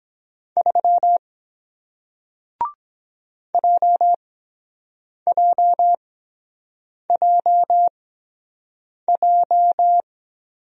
Morse code, heard as a keyed tone, about 23 wpm, spaced as for 7 wpm, 700 Hz. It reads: 3 JJJJ